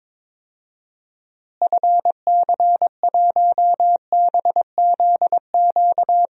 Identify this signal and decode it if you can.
Morse code, heard as a keyed tone, about 22 wpm, 700 Hz. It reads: FC1BZQ
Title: DXCC-100 Call Signs - 6 Characters - Repeated 2x - 22wpm